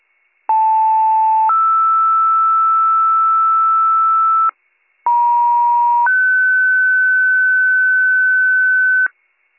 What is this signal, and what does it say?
Two-tone page, heard as a one-second tone, then a three-second tone, about 880 then 1320 Hz, about 950 then 1500 Hz, with noise. It is heard from 0.5 to 9.1 s.